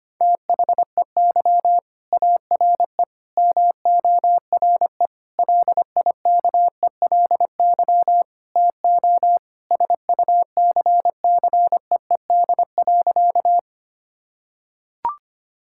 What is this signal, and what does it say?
Morse code, heard as a keyed tone, about 25 wpm, 700 Hz. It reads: THEY ARE MORE LIKELY TO SUCCEED.
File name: practice / 25wpm / Sentences with 4-Word Phrases - 25wpm